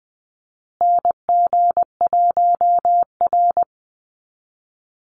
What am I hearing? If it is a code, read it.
Morse code, heard as a keyed tone, about 20 wpm, 700 Hz. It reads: NG1R
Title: Call Signs Repeated 2x - 20wpm